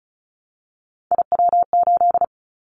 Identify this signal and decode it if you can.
Morse code, heard as a keyed tone, about 34 wpm, 700 Hz. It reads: IW8
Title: Callsign Prefixes for Top 25 DXCC - Repeated 3x - 35wpm